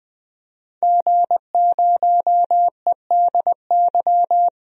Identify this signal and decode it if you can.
Morse code, heard as a keyed tone, about 20 wpm, 700 Hz. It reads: G0EDY
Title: DXCC-100 Call Signs - 5 Characters - 20wpm